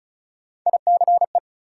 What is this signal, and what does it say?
Morse code, heard as a keyed tone, about 36 wpm, 700 Hz. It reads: ICE
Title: Sentences from Top 1000 Words - New Word Drill - 35wpm